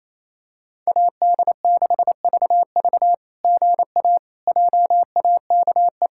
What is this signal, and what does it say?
Morse code, heard as a keyed tone, about 28 wpm, 700 Hz. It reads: AD6VV GA JAKE
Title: SOTA QSO Practice 28wpm